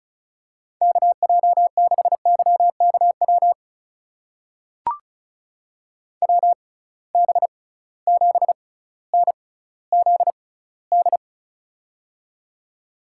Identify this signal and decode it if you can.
Morse code, heard as a keyed tone, about 34 wpm, spaced as for 18 wpm, 700 Hz. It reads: KJ6YKW WB7NZD
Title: USA Call Signs - 6 Characters - 6x Character Spacing - 35wpm